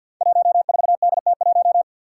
Morse code, heard as a keyed tone, about 51 wpm, 700 Hz. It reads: 14DT1
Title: Five Challenging Letter-Number Combinations 50wpm